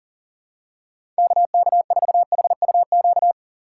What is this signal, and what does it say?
Morse code, heard as a keyed tone, about 40 wpm, 700 Hz. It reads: KK4HUQ
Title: USA Call Signs - 6 Characters - 3x Character Spacing - 40wpm